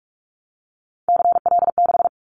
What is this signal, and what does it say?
Morse code, heard as a keyed tone, about 46 wpm, 700 Hz. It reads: CL6
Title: Callsign Prefixes for Top 100 DXCC - 45wpm